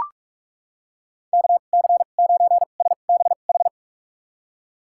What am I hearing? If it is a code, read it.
Morse code, heard as a keyed tone, about 44 wpm, spaced as for 32 wpm, 700 Hz. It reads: KC9SBH
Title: USA Call Signs - 6 Characters - 2x Character Spacing - 45wpm